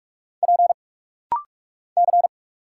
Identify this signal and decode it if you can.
Morse code, heard as a keyed tone, about 46 wpm, 700 Hz. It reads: P C
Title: Single Letters - Rapid-Fire 45wpm